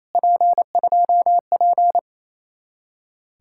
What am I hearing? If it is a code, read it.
Morse code, heard as a keyed tone, about 37 wpm, 700 Hz. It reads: P2P